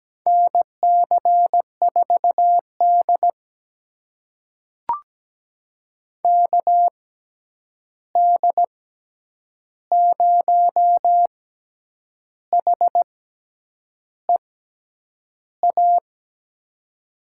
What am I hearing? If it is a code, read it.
Morse code, heard as a keyed tone, about 17 wpm, spaced as for 8 wpm, 700 Hz. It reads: NC4D KD0HEA
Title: USA Call Signs - 6x Character Spacing - 17wpm